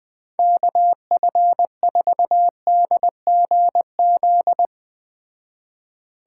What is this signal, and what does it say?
Morse code, heard as a keyed tone, about 20 wpm, 700 Hz. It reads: KF4DGZ